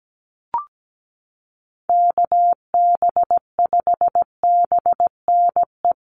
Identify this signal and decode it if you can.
Morse code, heard as a keyed tone, about 17 wpm, 700 Hz. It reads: KB5BNE
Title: USA Call Signs Repeated 3x 17wpm